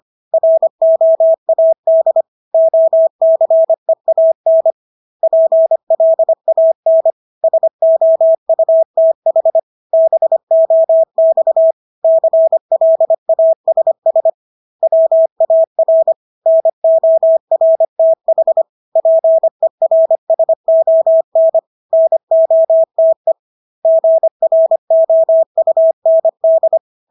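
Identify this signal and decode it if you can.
Morse code, heard as a keyed tone, about 25 wpm, 650 Hz. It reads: ROAD OCEAN PLAN SOUTH BOX CLASS WAR NORTH PERSON NOTE GROUND
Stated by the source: Morse practice of English words at 25 wpm from CW med Gnister.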